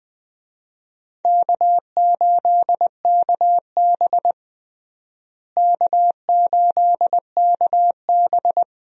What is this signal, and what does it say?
Morse code, heard as a keyed tone, about 21 wpm, 700 Hz. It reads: K8KB K8KB
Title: USA Call Signs - 4 Characters - Repeated 3x - 20wpm